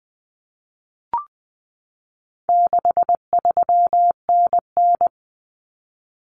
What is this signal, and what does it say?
Morse code, heard as a keyed tone, about 20 wpm, 700 Hz. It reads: 63NN
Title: Common Cut Numbers - 4-Digit 20wpm